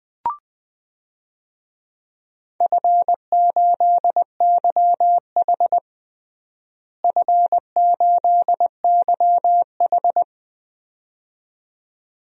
Morse code, heard as a keyed tone, about 21 wpm, 700 Hz. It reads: F8YH F8YH
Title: DXCC-100 Call Signs - 4 Characters - Repeated 2x - 20wpm